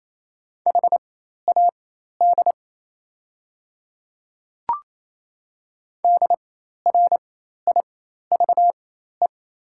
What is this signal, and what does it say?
Morse code, heard as a keyed tone, about 28 wpm, spaced as for 13 wpm, 700 Hz. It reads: HAD DRIVE
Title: 4x Character Spacing - Top 500 Words - 28wpm